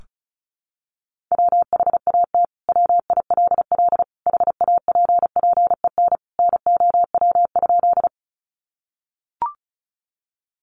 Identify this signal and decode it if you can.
Morse code, heard as a keyed tone, about 34 wpm, 700 Hz. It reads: WHAT WILL HAPPEN NOW?
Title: Sentences from Top 400 Words 35wpm